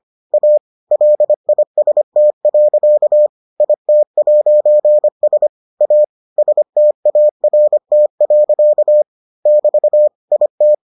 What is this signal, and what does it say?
Morse code, heard as a keyed tone, about 25 wpm, 600 Hz. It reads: A LIST. IT'S A START. <BT> IT